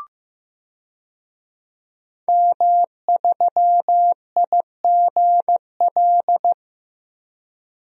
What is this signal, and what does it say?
Morse code, heard as a keyed tone, about 15 wpm, 700 Hz. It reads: M3IGL
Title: DXCC-100 Call Signs - 5 Characters - Repeated 2x - 15wpm